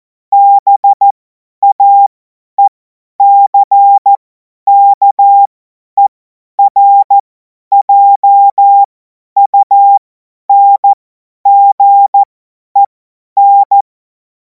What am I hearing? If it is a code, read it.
Morse code, heard as a keyed tone, about 14 wpm, spaced as for 10 wpm, 800 Hz. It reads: BAECKERJUNGEN
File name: Morsekurs / Witze / Witz 10. März 2026 50 F70